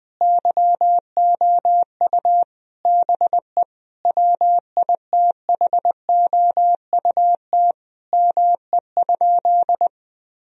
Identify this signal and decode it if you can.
Morse code, heard as a keyed tone, about 20 wpm, 700 Hz. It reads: YOU BE WITHOUT ME?
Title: Sentences from Top 1000 Words - Review Part 3 - 20wpm